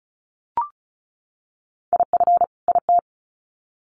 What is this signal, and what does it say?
Morse code, heard as a keyed tone, about 34 wpm, 700 Hz. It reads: IF IT
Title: Sets of 2 Words 35wpm